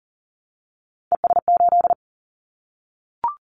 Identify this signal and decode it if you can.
Morse code, heard as a keyed tone, about 38 wpm, 700 Hz. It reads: ES8
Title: Callsign Prefixes for Top 100 DXCC - 40wpm